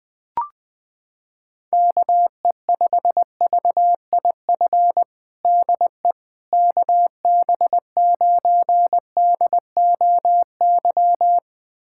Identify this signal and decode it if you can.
Morse code, heard as a keyed tone, about 20 wpm, 700 Hz. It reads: KE5VIF DE KB9DOY
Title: USA Callsign DE Callsign - 6 Characters - 20wpm